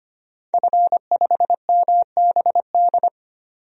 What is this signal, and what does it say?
Morse code, heard as a keyed tone, about 25 wpm, 700 Hz. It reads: F5MBD
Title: DXCC-100 Call Signs - 5 Characters - 25wpm